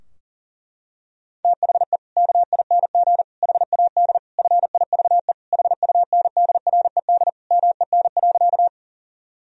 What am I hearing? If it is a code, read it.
Morse code, heard as a keyed tone, about 37 wpm, 700 Hz. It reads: THE KING HAD FIVE HUNDRED MEN.
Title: Sentences from Top 1000 Words - Review Part 2 - 40wpm